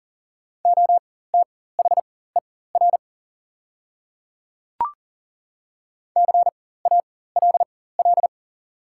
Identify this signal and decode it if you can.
Morse code, heard as a keyed tone, about 41 wpm, spaced as for 19 wpm, 700 Hz. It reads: OTHER CALL
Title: 4x Character Spacing - Top 200 Words - 40wpm